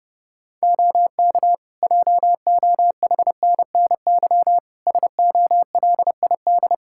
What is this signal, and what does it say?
Morse code, heard as a keyed tone, about 30 wpm, 700 Hz. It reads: OK JOHNNY SOLID
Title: Practice QSO - Exchange 3 - More Info - 30wpm